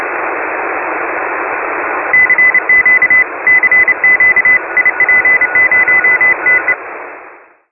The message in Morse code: CQ CQ IP0TE

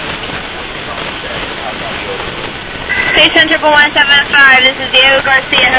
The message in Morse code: none

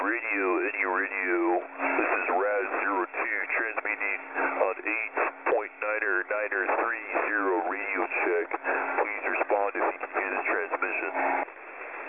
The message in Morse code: none

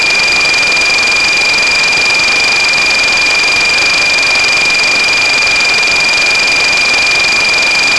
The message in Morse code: none

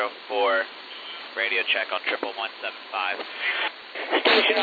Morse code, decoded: none

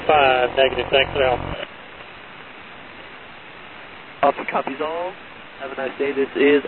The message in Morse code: none

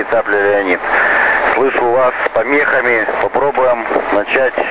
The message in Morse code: none

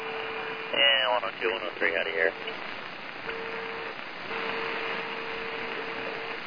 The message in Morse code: none

